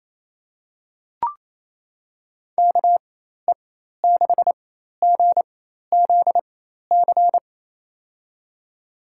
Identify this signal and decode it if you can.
Morse code, heard as a keyed tone, about 28 wpm, spaced as for 13 wpm, 700 Hz. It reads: KE6GZC
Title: USA Call Signs - 6 Characters - 4x Character Spacing - 28wpm